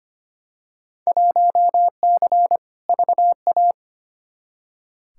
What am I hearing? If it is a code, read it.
Morse code, heard as a keyed tone, about 25 wpm, 700 Hz. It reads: 1C VA
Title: Field Day Exchanges Encore - 25wpm